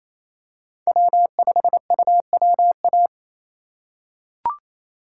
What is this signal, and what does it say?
Morse code, heard as a keyed tone, about 28 wpm, 700 Hz. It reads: W5UWA
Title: USA Call Signs - 6x Character Spacing - 28wpm